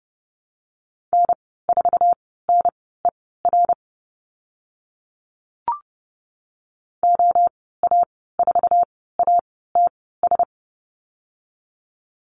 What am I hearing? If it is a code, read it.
Morse code, heard as a keyed tone, about 30 wpm, spaced as for 17 wpm, 700 Hz. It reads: N4NER OA4ATS
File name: Lesson 14g - Number 2 - Call Signs 30wpm